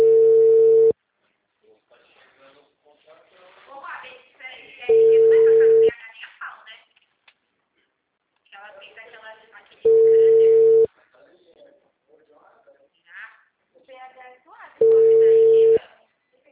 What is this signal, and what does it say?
Ringback tone (China, 450 Hz), with speech.
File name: out- -2022- - - .83356